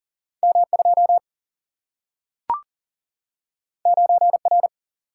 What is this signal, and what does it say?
Morse code, heard as a keyed tone, about 40 wpm, 700 Hz. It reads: M2 9R
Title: Single-Letter Single-Number 40wpm